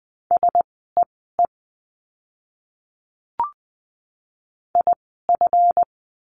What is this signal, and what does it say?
Morse code, heard as a keyed tone, about 20 wpm, spaced as for 15 wpm, 700 Hz. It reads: SEE IF
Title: 160 Lesson 26f - Letter g - Words 20wpm